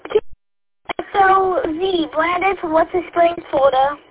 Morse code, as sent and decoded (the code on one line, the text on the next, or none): none